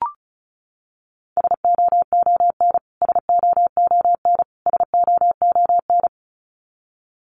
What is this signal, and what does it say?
Morse code, heard as a keyed tone, about 34 wpm, 700 Hz. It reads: SOON SOON SOON
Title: Top 500 Words Repeated 35wpm